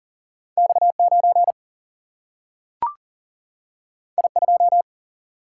Morse code, heard as a keyed tone, about 39 wpm, 700 Hz. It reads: X9 I2
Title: Single-Letter Single-Number 40wpm